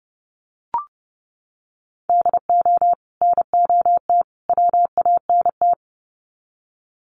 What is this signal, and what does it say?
Morse code, heard as a keyed tone, about 30 wpm, 700 Hz. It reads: DO NOT WANT
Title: Sets of 3 Words 30wpm